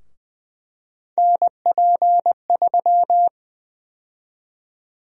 Morse code, heard as a keyed tone, about 21 wpm, 700 Hz. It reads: NP3
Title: USA Call Sign Prefixes - 20wpm